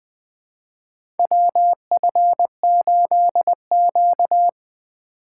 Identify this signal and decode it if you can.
Morse code, heard as a keyed tone, about 20 wpm, 700 Hz. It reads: WF8Q
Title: USA Call Signs - 4 Characters - 2x Character Spacing - 20wpm